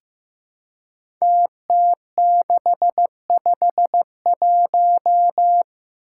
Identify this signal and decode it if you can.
Morse code, heard as a keyed tone, about 15 wpm, 700 Hz. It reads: TT651
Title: Five Challenging Letter-Number Combinations 15wpm